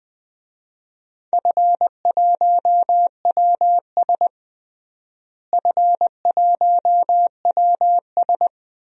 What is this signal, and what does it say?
Morse code, heard as a keyed tone, about 20 wpm, 690 Hz. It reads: F1WS F1WS
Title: DXCC-100 Call Signs - 4 Characters - Repeated 3x - 20wpm